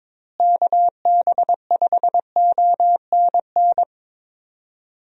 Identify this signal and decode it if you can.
Morse code, heard as a keyed tone, about 22 wpm, 700 Hz. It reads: KB5ONN
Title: USA Call Signs - 6 Characters - Repeated 3x - 22wpm